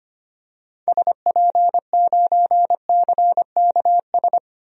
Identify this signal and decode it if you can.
Morse code, heard as a keyed tone, about 25 wpm, 700 Hz. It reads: SP9CKS
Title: DXCC-100 Call Signs - 6 Characters - 25wpm